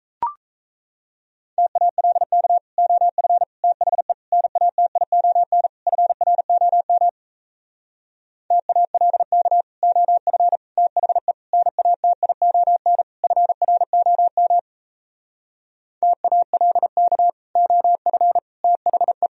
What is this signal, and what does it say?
Morse code, heard as a keyed tone, about 37 wpm, 700 Hz. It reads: TALK OF THE NATION FROM TALK OF THE NATION FROM TALK OF THE